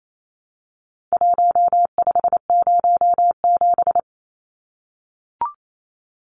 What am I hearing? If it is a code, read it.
Morse code, heard as a keyed tone, about 28 wpm, 700 Hz. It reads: 1507
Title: Numbers - Contest Style - 4-Digit 28wpm